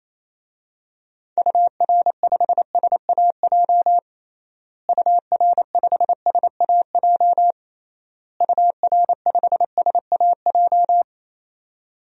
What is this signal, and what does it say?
Morse code, heard as a keyed tone, about 28 wpm, 700 Hz. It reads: UR5SAJ UR5SAJ UR5SAJ
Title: DXCC-100 Call Signs - 6 Characters - Repeated 3x - 28wpm